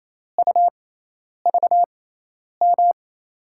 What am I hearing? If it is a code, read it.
Morse code, heard as a keyed tone, about 28 wpm, spaced as for 10 wpm, 700 Hz. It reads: UVM
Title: USA Call Signs - 6x Character Spacing - 28wpm